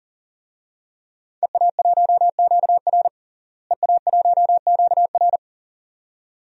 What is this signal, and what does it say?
Morse code, heard as a keyed tone, about 40 wpm, 690 Hz. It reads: EA1QR EA1QR
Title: DXCC-100 Call Signs - 5 Characters - Repeated 2x - 40wpm